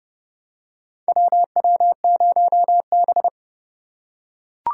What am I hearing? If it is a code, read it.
Morse code, heard as a keyed tone, about 32 wpm, 700 Hz. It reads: WW0B
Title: USA Call Signs - 4 Characters - 3x Character Spacing - 30wpm